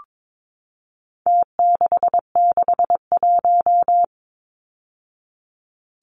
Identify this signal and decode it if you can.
Morse code, heard as a keyed tone, about 22 wpm, 700 Hz. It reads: T661
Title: Numbers - Contest Style - 4-Digit 22wpm